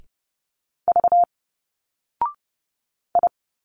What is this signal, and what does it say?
Morse code, heard as a keyed tone, about 30 wpm, 700 Hz. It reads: V I
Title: Single Letters 30wpm